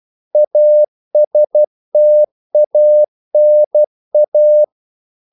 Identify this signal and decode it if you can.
Morse code, heard as a keyed tone, about 12 wpm, 600 Hz. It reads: ASTANA
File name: CW / Practice / NationalCapitals12